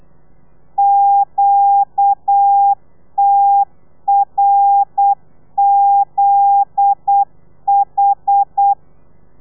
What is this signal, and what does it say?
Morse code, heard as a keyed tone, about 8 wpm, 800 Hz. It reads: QTRZH
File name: telegrafia / i2tfj / Lez9 35